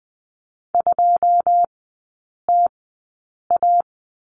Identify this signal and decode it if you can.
Morse code, heard as a keyed tone, about 20 wpm, spaced as for 8 wpm, 690 Hz. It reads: 2TA
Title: 081 Lesson 14e - Number 2 - Three Characters 20wpm